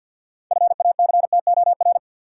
Morse code, heard as a keyed tone, about 52 wpm, 700 Hz. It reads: FACTOR
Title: Sentences from Top 1300 Words - New Word Drill Alt - 50wpm